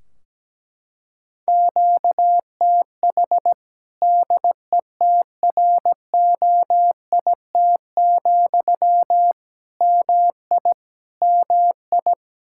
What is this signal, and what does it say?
Morse code, heard as a keyed tone, about 17 wpm, 700 Hz. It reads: QTH DETROIT, MI MI
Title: Practice QSO - Exchange 1 - Beginning - 17wpm